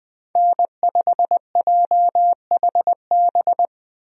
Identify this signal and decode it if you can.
Morse code, heard as a keyed tone, about 20 wpm, 700 Hz. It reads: N5JHB
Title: USA Call Signs - 5 Characters - Repeated 3x - 20wpm